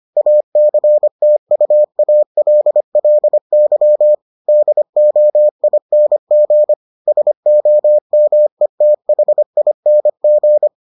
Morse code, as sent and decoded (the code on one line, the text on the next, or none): .- -.-. - ..- .- .-.. .-.. -.-- / -.. --- .. -. --. / ... --- -- . - .... .. -. --.
ACTUALLY DOING SOMETHING